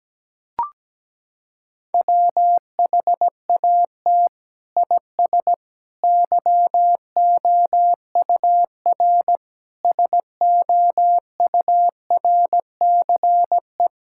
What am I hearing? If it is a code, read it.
Morse code, heard as a keyed tone, about 17 wpm, 700 Hz. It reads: WHAT IS YOUR SOURCE
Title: Sentences from Top 1000 Words - Review Part 5 - 17wpm